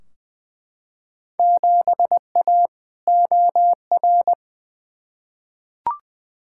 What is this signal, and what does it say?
Morse code, heard as a keyed tone, about 21 wpm, 700 Hz. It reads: 7A OR
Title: Field Day Exchanges Encore - 20wpm